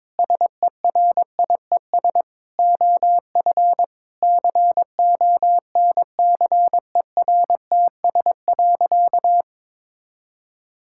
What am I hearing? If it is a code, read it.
Morse code, heard as a keyed tone, about 22 wpm, 700 Hz. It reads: SERIES OF CONCERTS.
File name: practice / 22wpm / Sentences with 8-Word Phrases - 22wpm